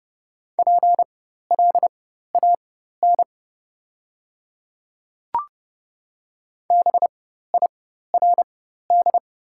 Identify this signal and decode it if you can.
Morse code, heard as a keyed tone, about 29 wpm, spaced as for 14 wpm, 690 Hz. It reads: PLAN BIRD
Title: 4x Character Spacing - Top 500 Words - 30wpm